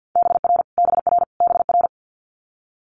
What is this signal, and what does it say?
Morse code, heard as a keyed tone, about 50 wpm, 700 Hz. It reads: BR BR BR
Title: Multi-letter Phonemes Repeated 3x 50wpm